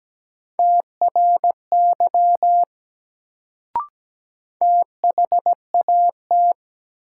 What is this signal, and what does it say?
Morse code, heard as a keyed tone, about 17 wpm, 700 Hz. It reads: TRY THAT
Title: Sets of 2 Words - New Word Drill - 17wpm